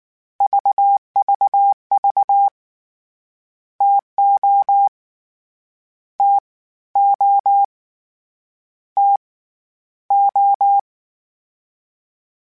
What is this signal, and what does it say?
Morse code, heard as a keyed tone, about 19 wpm, 800 Hz. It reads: VVV TO T O T O
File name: Variable Speed / Variable Speed-mp3s / TO sp @18